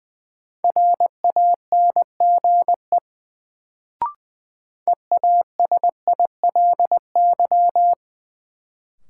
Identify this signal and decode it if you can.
Morse code, heard as a keyed tone, about 20 wpm, 700 Hz. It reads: RANGE EASILY
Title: Sentences from Top 900 Words - New Word Drill - 20wpm